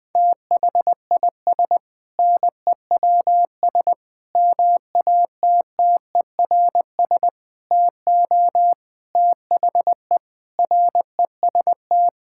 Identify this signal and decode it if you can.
Morse code, heard as a keyed tone, about 20 wpm, 700 Hz. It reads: THIS NEWS MATTERS TO THE REST
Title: Sentences with 6-Word Phrases - 20wpm